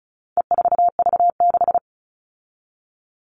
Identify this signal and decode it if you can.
Morse code, heard as a keyed tone, about 34 wpm, 700 Hz. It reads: E4V6
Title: All Cut Numbers - 4-Digit 35wpm